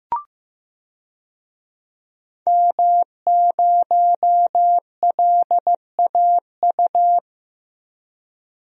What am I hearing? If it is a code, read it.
Morse code, heard as a keyed tone, about 15 wpm, 700 Hz. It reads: M0LAU
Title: DXCC-100 Call Signs - 5 Characters - 15wpm